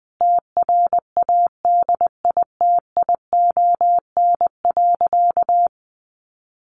Morse code, heard as a keyed tone, about 20 wpm, 700 Hz. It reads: TRADITION.
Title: Sentences from Top 1500 Words 20wpm